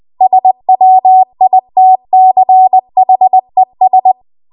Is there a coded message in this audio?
Yes: Morse code from 0.2 to 4.1 s, heard as a keyed tone, about 20 wpm, 750 Hz. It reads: SWITCHES